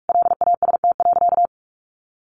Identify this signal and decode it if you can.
Morse code, heard as a keyed tone, about 44 wpm, 700 Hz. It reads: LAST.
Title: Sentences from Top 1100 Words 45wpm